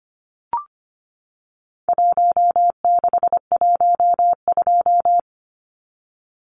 Morse code, heard as a keyed tone, about 25 wpm, 700 Hz. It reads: 1612
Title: Numbers - Contest Style - 4-Digit 25wpm